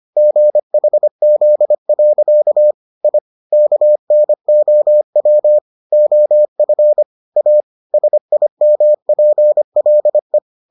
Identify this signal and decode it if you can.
Morse code, heard as a keyed tone, about 25 wpm, 600 Hz. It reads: GHZ. I KNOW OF A SIMPLE